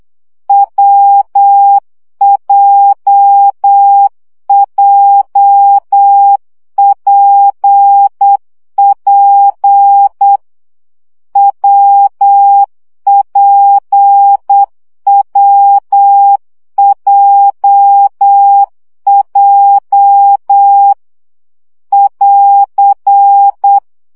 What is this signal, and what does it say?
Morse code, heard as a keyed tone, about 8 wpm, 800 Hz. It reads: WJJPP WPWJJ <AR>
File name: CW gyakorlas w p j 7 baud